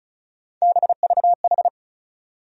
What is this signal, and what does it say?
Morse code, heard as a keyed tone, about 36 wpm, 700 Hz. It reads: BVH
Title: USA Call Sign Postfixes - 3 Letters 35wpm